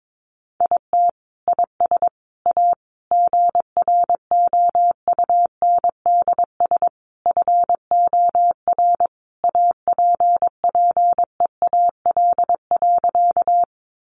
Morse code, heard as a keyed tone, about 22 wpm, 700 Hz. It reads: IT IS A GROUNDS FOR APPEAL.
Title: Sentences from Top 1300 Words 22wpm